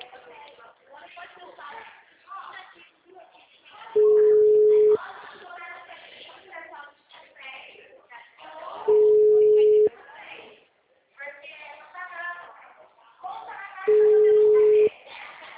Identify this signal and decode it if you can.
Ringback tone (Europe, 425 Hz), with speech.